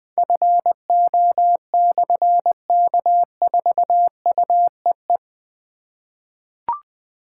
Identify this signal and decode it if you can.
Morse code, heard as a keyed tone, about 20 wpm, 700 Hz. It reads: FO/K4UEE
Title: Difficult Call Signs 20wpm